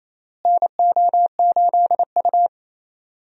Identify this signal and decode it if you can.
Morse code, heard as a keyed tone, about 27 wpm, 700 Hz. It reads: NO8U
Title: USA Call Signs - 4 Characters - Repeated 3x - 28wpm